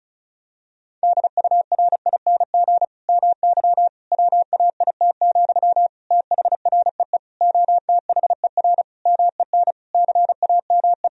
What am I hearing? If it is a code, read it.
Morse code, heard as a keyed tone, about 36 wpm, 700 Hz. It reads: DURING MY WAIT, THREE OTHER MEN CAME
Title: Sentences from Top 1000 Words - Review Part 2 - 35wpm